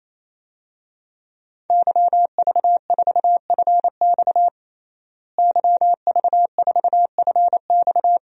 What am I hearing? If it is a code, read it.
Morse code, heard as a keyed tone, about 27 wpm, 700 Hz. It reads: YV4FX YV4FX
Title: DXCC-100 Call Signs - 5 Characters - Repeated 3x - 28wpm